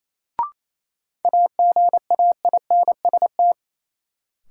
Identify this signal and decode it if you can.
Morse code, heard as a keyed tone, about 28 wpm, 700 Hz. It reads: AGAINST
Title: Sets of 2 Words - New Word Drill - 28wpm